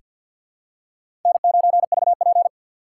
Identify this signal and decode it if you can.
Morse code, heard as a keyed tone, about 48 wpm, 700 Hz. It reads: N9VP
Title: USA Call Signs - 4 Characters - 2x Character Spacing - 50wpm